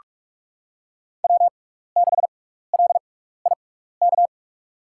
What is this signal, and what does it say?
Morse code, heard as a keyed tone, about 44 wpm, spaced as for 15 wpm, 700 Hz. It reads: W6LIK
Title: USA Call Signs - 5 Characters - 6x Character Spacing - 45wpm